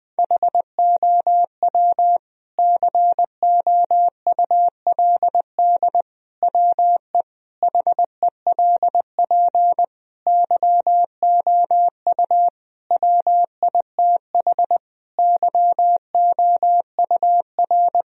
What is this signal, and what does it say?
Morse code, heard as a keyed tone, about 21 wpm, 700 Hz. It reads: HOW COULD WE HELP YOU WITH YOUR
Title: Sentences from Top 1000 Words - Review Part 3 - 20wpm